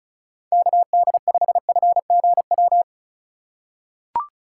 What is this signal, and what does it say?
Morse code, heard as a keyed tone, about 36 wpm, 700 Hz. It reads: KD5FGW
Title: USA Call Signs - 8x Character Spacing - 35wpm